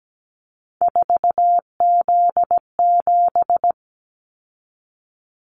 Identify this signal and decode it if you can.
Morse code, heard as a keyed tone, about 17 wpm, 700 Hz. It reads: 4Z7
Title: Callsign Prefixes for Top 50 DXCC - 17wpm